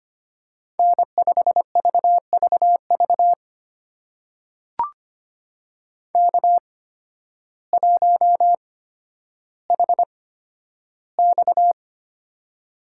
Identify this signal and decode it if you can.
Morse code, heard as a keyed tone, about 25 wpm, spaced as for 12 wpm, 700 Hz. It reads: N5VVV K1HX
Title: USA Call Signs - 8x Character Spacing - 25wpm